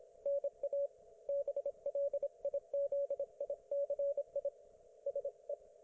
Morse code, heard as a keyed tone, about 27 wpm, 560 Hz. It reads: NA BLIZICI SE
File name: Archiv / CUC194 / ok1fhi cul194